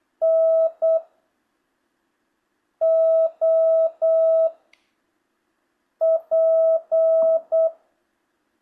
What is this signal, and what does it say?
Morse code, heard as a keyed tone, about 8 wpm, 650 Hz. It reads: N O P
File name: jh8cbh / 2025cw